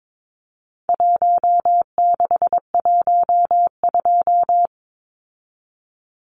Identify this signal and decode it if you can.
Morse code, heard as a keyed tone, about 22 wpm, 700 Hz. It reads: 1612
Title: Numbers - Contest Style - 4-Digit 22wpm